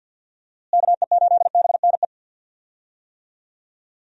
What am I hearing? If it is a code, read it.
Morse code, heard as a keyed tone, about 51 wpm, 700 Hz. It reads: KE8BNE
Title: USA Call Signs - 6 Characters - Repeated 2x - 50wpm